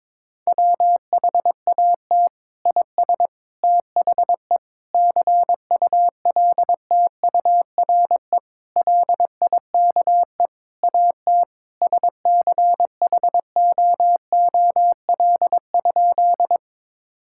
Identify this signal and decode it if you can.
Morse code, heard as a keyed tone, about 22 wpm, 700 Hz. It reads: WHAT IS THE CULTURE LIKE AT SCHOOL?
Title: Sentences from Top 1000 Words - Review Part 6 - 22wpm